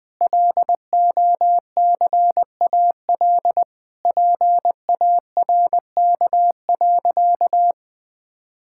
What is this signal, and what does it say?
Morse code, heard as a keyed tone, about 21 wpm, 700 Hz. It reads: LOCAL PARK.
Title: Difficult Sentences with Hints - 20wpm